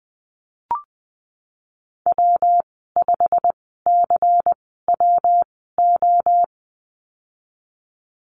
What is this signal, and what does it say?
Morse code, heard as a keyed tone, about 20 wpm, spaced as for 14 wpm, 700 Hz. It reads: W5CWO
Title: 154 Lesson 25g - Letter p - Call Signs 20wpm